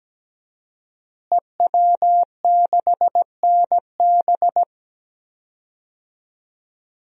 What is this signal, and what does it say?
Morse code, heard as a keyed tone, about 17 wpm, 700 Hz. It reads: EW6NB
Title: DXCC-100 Call Signs - 5 Characters - Repeated 3x - 17wpm